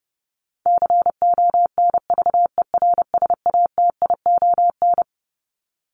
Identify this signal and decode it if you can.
Morse code, heard as a keyed tone, about 28 wpm, 700 Hz. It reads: CONVERSATION
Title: Top 1751-2000 Words 30wpm